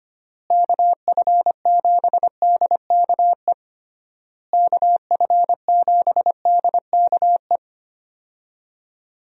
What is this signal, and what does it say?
Morse code, heard as a keyed tone, about 25 wpm, 700 Hz. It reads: KF7DKE KF7DKE
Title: USA Call Signs - 6 Characters - Repeated 3x - 25wpm